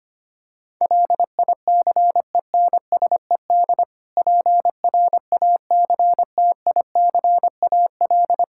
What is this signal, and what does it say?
Morse code, heard as a keyed tone, about 25 wpm, 700 Hz. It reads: LICENSED PRACTICAL
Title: Practice QSO - Exchange 3 - More Info - 25wpm